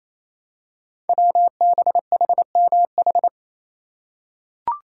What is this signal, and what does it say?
Morse code, heard as a keyed tone, about 27 wpm, 700 Hz. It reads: WBHMH
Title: Code Groups - 5 Letter:Numbers - 28wpm